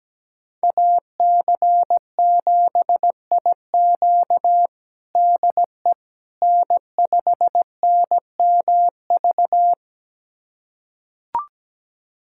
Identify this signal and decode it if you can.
Morse code, heard as a keyed tone, about 17 wpm, 700 Hz. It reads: AC7IQ DE N5NMV
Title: USA Callsign DE Callsign - 5 Characters - 17wpm